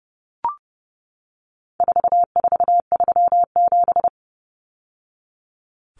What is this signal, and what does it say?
Morse code, heard as a keyed tone, about 30 wpm, 700 Hz. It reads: 4437
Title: Numbers - 4-Digit 30wpm